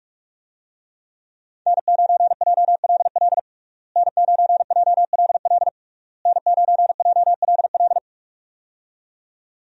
Morse code, heard as a keyed tone, about 41 wpm, 700 Hz. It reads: N9JLL N9JLL N9JLL
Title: USA Call Signs - 5 Characters - Repeated 3x - 45wpm